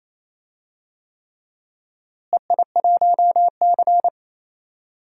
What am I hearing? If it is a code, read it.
Morse code, heard as a keyed tone, about 28 wpm, 700 Hz. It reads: EI1C
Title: DXCC-100 Call Signs - 4 Characters - Repeated 3x - 28wpm